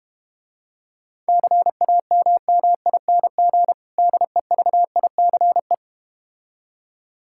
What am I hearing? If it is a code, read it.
Morse code, heard as a keyed tone, about 31 wpm, 700 Hz. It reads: CAMMING DEVICE